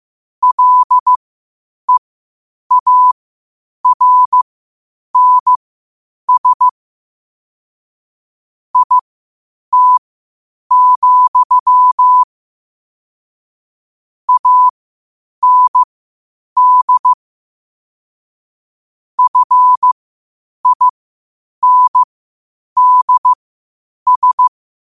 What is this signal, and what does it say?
Morse code, heard as a keyed tone, about 15 wpm, spaced as for 8 wpm, 1000 Hz. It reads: LEARNS IT, AND FINDS
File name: qotdfiles / 8wpm